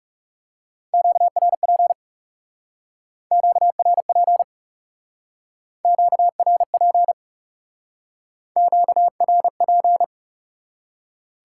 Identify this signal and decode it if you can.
Morse code, heard as a keyed tone, about 37 wpm, 700 Hz. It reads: QRP QRP QRP QRP
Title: Speed-Racing Top 90 QSO Elements 30wpm